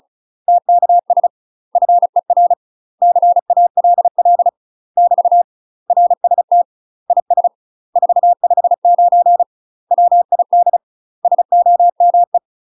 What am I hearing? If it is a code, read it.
Morse code, heard as a keyed tone, about 35 wpm, 700 Hz. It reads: TKS FER CALL <BT> RST IS 459 WID SOME
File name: UK MASTER QSOs TRAINING / 35WPM / UK MASTER QSOs 0039